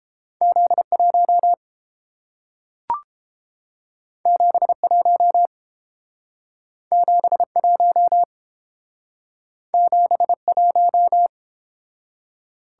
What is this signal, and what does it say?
Morse code, heard as a keyed tone, about 30 wpm, 700 Hz. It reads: Z1 71 71 71